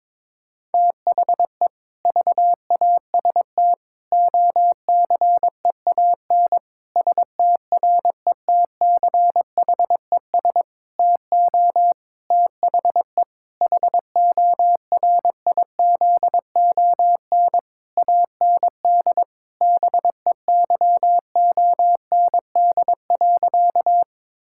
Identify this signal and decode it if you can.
Morse code, heard as a keyed tone, about 22 wpm, 700 Hz. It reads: THE VAST OCEAN STRETCHES TO THE HORIZON AND BEYOND.
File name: Difficult Sentences with Hints - 22wpm